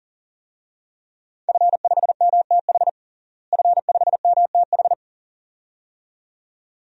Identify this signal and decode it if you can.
Morse code, heard as a keyed tone, about 37 wpm, 700 Hz. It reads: F5MTH F5MTH
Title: DXCC-100 Call Signs - 5 Characters - Repeated 2x - 40wpm